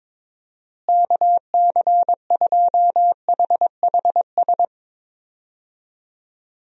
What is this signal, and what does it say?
Morse code, heard as a keyed tone, about 22 wpm, 700 Hz. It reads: KC2HHS
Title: USA Call Signs - 6 Characters - Repeated 2x - 22wpm